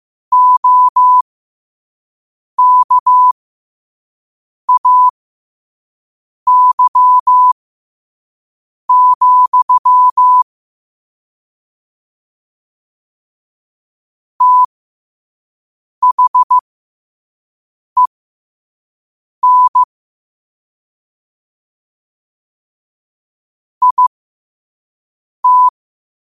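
Morse code, heard as a keyed tone, about 15 wpm, spaced as for 5 wpm, 1000 Hz. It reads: OKAY, THEN IT
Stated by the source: Morse code at 5 wpm.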